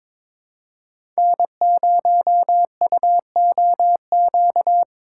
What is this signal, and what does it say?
Morse code, heard as a keyed tone, about 22 wpm, 700 Hz. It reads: N0UOQ
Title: USA Call Signs - 5 Characters - 2x Character Spacing - 22wpm